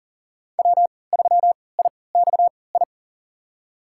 Morse code, heard as a keyed tone, about 40 wpm, spaced as for 23 wpm, 700 Hz. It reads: W3IXI
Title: USA Call Signs - 3x Character Spacing - 40wpm